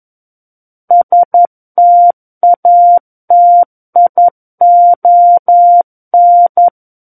Morse code, heard as a keyed tone, about 11 wpm, 700 Hz. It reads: STATION
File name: Book One / 02-Chapter 2 - 11 WPM